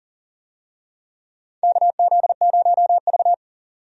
Morse code, heard as a keyed tone, about 40 wpm, 700 Hz. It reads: KZ0V
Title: USA Call Signs - 4 Characters - 40wpm